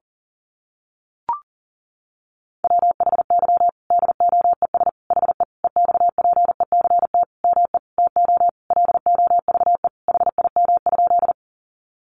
Morse code, heard as a keyed tone, about 40 wpm, 700 Hz. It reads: WHY DOES HE EXPECT ME TO LOVE HIM?
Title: Sentences from Top 1100 Words 40wpm